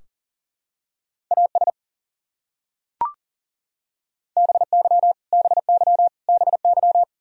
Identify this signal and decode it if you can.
Morse code, heard as a keyed tone, about 37 wpm, 700 Hz. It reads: AS BY BY BY
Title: Top 100 Words Repeated 40wpm